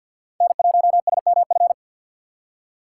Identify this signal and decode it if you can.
Morse code, heard as a keyed tone, about 52 wpm, 700 Hz. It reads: N1SMF
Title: USA Call Signs - 5 Characters - 4x Character Spacing - 50wpm